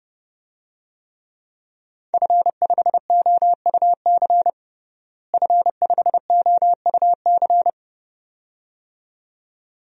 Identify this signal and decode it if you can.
Morse code, heard as a keyed tone, about 31 wpm, 700 Hz. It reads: F5OUC F5OUC
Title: DXCC-100 Call Signs - 5 Characters - Repeated 2x - 30wpm